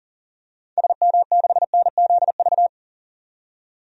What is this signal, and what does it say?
Morse code, heard as a keyed tone, about 40 wpm, 700 Hz. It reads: SM6NZV
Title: DXCC-100 Call Signs - 6 Characters - 40wpm